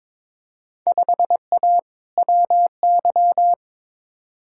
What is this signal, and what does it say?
Morse code, heard as a keyed tone, about 21 wpm, 700 Hz. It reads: 5A WY
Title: Field Day Exchanges Encore - 22wpm